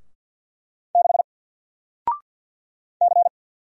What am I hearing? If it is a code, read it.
Morse code, heard as a keyed tone, about 48 wpm, 700 Hz. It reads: 6 C
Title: Single Character (Letter:Number:Punctuation:Prosign) Alt 50wpm